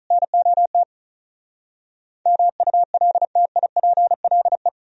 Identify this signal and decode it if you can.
Morse code, heard as a keyed tone, about 46 wpm, 700 Hz. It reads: NOT MULTIPLE